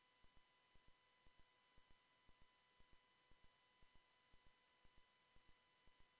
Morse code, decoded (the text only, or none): none